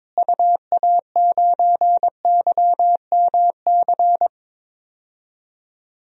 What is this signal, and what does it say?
Morse code, heard as a keyed tone, about 22 wpm, 700 Hz. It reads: UA9YMC
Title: DXCC-100 Call Signs - 6 Characters - Repeated 2x - 22wpm